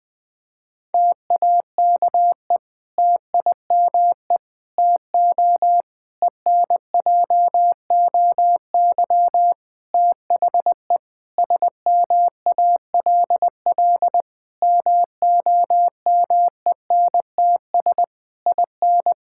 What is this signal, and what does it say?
Morse code, heard as a keyed tone, about 20 wpm, 700 Hz. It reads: TAKE TIME TO ENJOY THE SMALL MOMENTS IN
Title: Difficult Sentences with Hints - 20wpm